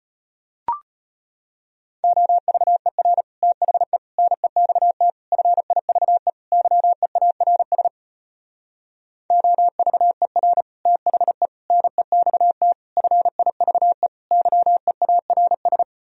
Morse code, heard as a keyed tone, about 33 wpm, 700 Hz. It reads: OVER THE NEXT FIVE YEARS OVER THE NEXT FIVE YEARS